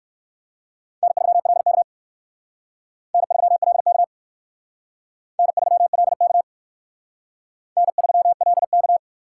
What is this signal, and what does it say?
Morse code, heard as a keyed tone, about 58 wpm, 700 Hz. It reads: N3LK N3LK N3LK N3LK